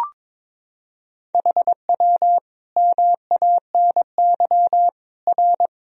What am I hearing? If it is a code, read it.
Morse code, heard as a keyed tone, about 22 wpm, 700 Hz. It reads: HW MANY R
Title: Sentences from Top 300 Words and Abbreviations 22wpm